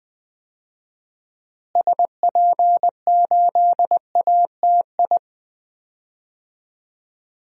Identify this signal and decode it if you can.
Morse code, heard as a keyed tone, about 20 wpm, 700 Hz. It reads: SP8ATI
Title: DXCC-100 Call Signs - 6 Characters - Repeated 3x - 20wpm